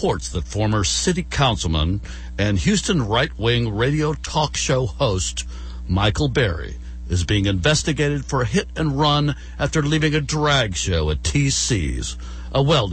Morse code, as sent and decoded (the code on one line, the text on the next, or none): none